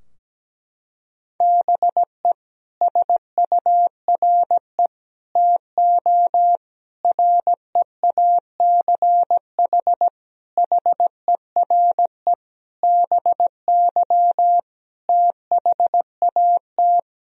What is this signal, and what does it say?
Morse code, heard as a keyed tone, about 17 wpm, 700 Hz. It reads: BE SURE TO REACH HERE BY THAT
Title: Sentences from Top 1000 Words - Review Part 2 - 17wpm